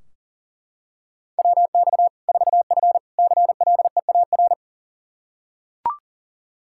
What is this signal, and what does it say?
Morse code, heard as a keyed tone, about 40 wpm, 700 Hz. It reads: WX 4F CLEAR
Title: Practice QSO - Exchange 2 - Info - 40wpm